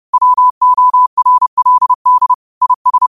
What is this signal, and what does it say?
Morse code, heard as a keyed tone, about 30 wpm, 1000 Hz. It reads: WORLD IS
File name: qotdfiles / 30wpm